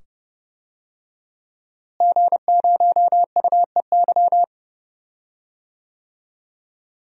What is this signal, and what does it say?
Morse code, heard as a keyed tone, about 30 wpm, 700 Hz. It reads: G0UEY
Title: DXCC-100 Call Signs - 5 Characters - Repeated 2x - 30wpm